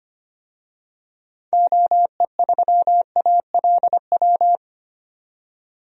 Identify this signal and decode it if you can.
Morse code, heard as a keyed tone, about 25 wpm, 700 Hz. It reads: OE3ALW
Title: DXCC-100 Call Signs - 6 Characters - 25wpm